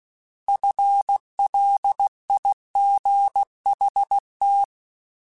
Morse code, heard as a keyed tone, about 16 wpm, 800 Hz. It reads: FLIGHT
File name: Alphabetical / Alphabetical mp3s / Fs @15